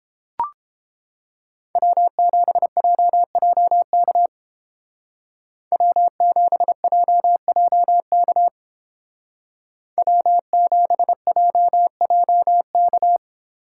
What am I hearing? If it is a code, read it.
Morse code, heard as a keyed tone, about 31 wpm, 700 Hz. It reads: W7JJK W7JJK W7JJK